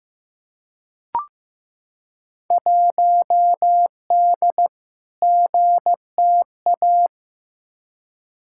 Morse code, heard as a keyed tone, about 15 wpm, 700 Hz. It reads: 1D GTA
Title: Field Day Exchanges - 15wpm